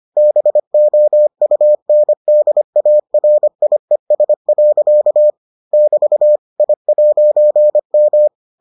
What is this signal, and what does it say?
Morse code, heard as a keyed tone, about 25 wpm, 600 Hz. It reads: BOUNDARIES. <BT> I'M